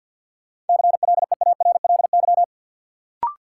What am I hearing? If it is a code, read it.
Morse code, heard as a keyed tone, about 52 wpm, 700 Hz. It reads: CLEARLY
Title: Sentences from Top 1300 Words - New Word Drill - 50wpm